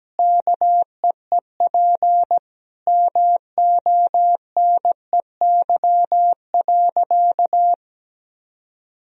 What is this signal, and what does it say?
Morse code, heard as a keyed tone, about 17 wpm, 700 Hz. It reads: KEEP MONEY.
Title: Sentences from Top 1000 Words - Review Part 4 - 17wpm